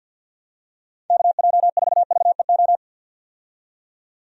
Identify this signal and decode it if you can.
Morse code, heard as a keyed tone, about 50 wpm, 700 Hz. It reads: KJ4VEO
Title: USA Call Signs - 6 Characters - 50wpm